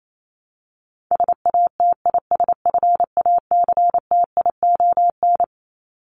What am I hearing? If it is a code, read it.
Morse code, heard as a keyed tone, about 28 wpm, 700 Hz. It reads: SATISFACTION
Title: Top 1751-2000 Words 28wpm